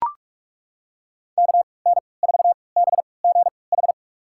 Morse code, heard as a keyed tone, about 43 wpm, spaced as for 25 wpm, 700 Hz. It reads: KN4BGH
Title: USA Call Signs - 6 Characters - 3x Character Spacing - 45wpm